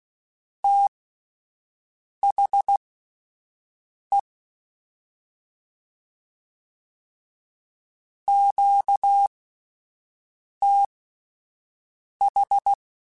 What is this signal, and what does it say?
Morse code, heard as a keyed tone, about 16 wpm, spaced as for 5 wpm, 800 Hz. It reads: THE QTH
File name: (spaced) / QSOs Spaced-mp3s / WA7QOX 2 @10 sp